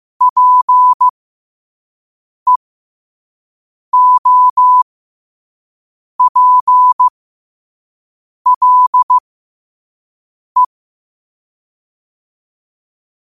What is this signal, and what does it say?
Morse code, heard as a keyed tone, about 15 wpm, spaced as for 5 wpm, 1000 Hz. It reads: PEOPLE
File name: qotdfiles / 5wpm